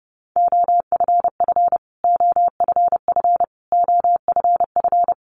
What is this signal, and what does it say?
Morse code, heard as a keyed tone, about 30 wpm, 700 Hz. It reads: OFF OFF OFF
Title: Top 500 Words Repeated 30wpm